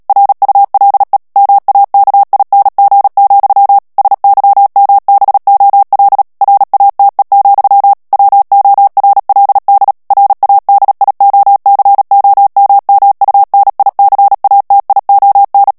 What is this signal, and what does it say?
Morse code, heard as a keyed tone, about 37 wpm, 800 Hz. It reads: RULE MAKING, SYMBOL RATE, WORLD RADIOCOMMUNICATION
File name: morse mp3 / ANGLAIS texte clair 3 vitesse 35 mots minute